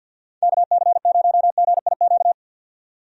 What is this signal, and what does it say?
Morse code, heard as a keyed tone, about 50 wpm, 700 Hz. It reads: KC0GIQ
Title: USA Call Signs - 6 Characters - 50wpm